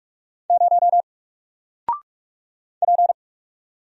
Morse code, heard as a keyed tone, about 44 wpm, 700 Hz. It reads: 0 P
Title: Single Character (Letter:Number:Punctuation:Prosign) Alt 45wpm